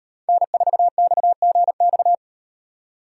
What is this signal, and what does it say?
Morse code, heard as a keyed tone, about 35 wpm, 700 Hz. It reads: N4XGX